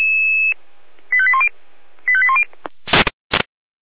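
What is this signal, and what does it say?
Five-tone selective call (ZVEI), heard as quick runs of five tones, tones of about 70 ms. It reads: 85810 85810
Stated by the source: Five-tone ZVEI selective call received on an SDR.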